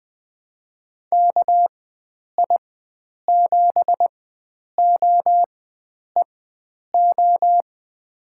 Morse code, heard as a keyed tone, about 21 wpm, spaced as for 9 wpm, 700 Hz. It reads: KI7OEO
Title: USA Call Signs - 6 Characters - 4x Character Spacing - 20wpm